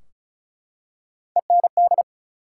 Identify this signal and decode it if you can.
Morse code, heard as a keyed tone, about 36 wpm, 700 Hz. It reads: END
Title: Three Letter Words - 35wpm